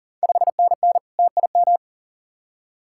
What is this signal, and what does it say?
Morse code, heard as a keyed tone, about 38 wpm, 690 Hz. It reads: 5NN TIM
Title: Ontario QSO Party Exchange 40wpm